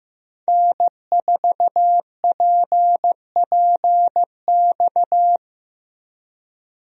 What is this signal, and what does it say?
Morse code, heard as a keyed tone, about 15 wpm, 700 Hz. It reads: N4PPX
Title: USA Call Signs - 5 Characters - Repeated 2x - 15wpm